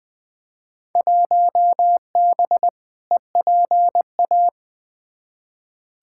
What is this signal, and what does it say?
Morse code, heard as a keyed tone, about 20 wpm, 700 Hz. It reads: 1B EPA
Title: Field Day Exchanges Encore - 20wpm